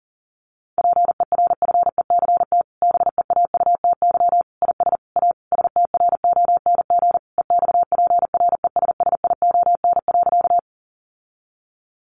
Morse code, heard as a keyed tone, about 40 wpm, 700 Hz. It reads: PERFECT BEAUTY IS A STRONG EXPRESSION.
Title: Sentences from Top 1200 Words 40wpm